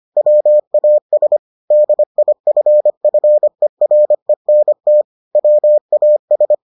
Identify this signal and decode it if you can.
Morse code, heard as a keyed tone, about 25 wpm, 600 Hz. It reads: WAS DIFFERENT WAS